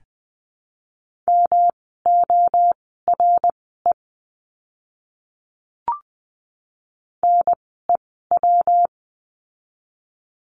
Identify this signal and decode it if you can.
Morse code, heard as a keyed tone, about 20 wpm, spaced as for 14 wpm, 700 Hz. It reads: MORE NEW
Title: 237 Lesson 38f - Letter q - Words 20wpm